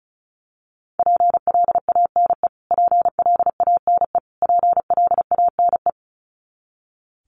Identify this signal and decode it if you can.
Morse code, heard as a keyed tone, about 34 wpm, 700 Hz. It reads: PLANE PLANE PLANE
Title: Top 500 Words Repeated 35wpm